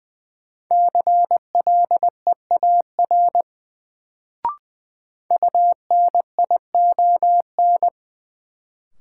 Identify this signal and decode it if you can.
Morse code, heard as a keyed tone, about 20 wpm, 700 Hz. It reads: CLEAR UNION
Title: Sets of 5 Words Encore - New Word Drill - 20wpm